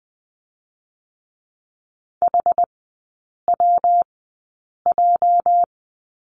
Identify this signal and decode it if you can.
Morse code, heard as a keyed tone, about 20 wpm, 700 Hz. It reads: H W J
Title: 205 Lesson 33e - Letter j - Three Characters 20wpm